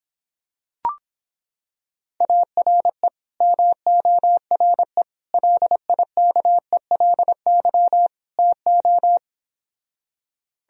Morse code, heard as a keyed tone, about 26 wpm, 700 Hz. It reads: ARE MORE LIKELY TO